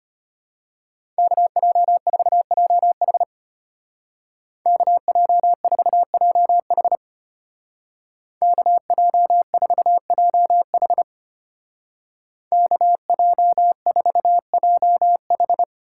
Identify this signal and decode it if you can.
Morse code, heard as a keyed tone, about 32 wpm, 700 Hz. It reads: KJ4JH KJ4JH KJ4JH KJ4JH